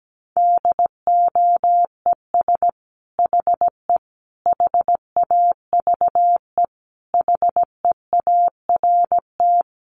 Morse code, heard as a keyed tone, about 17 wpm, 700 Hz. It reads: DOES HE HAVE HEART
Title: Sentences from Top 1200 Words 17wpm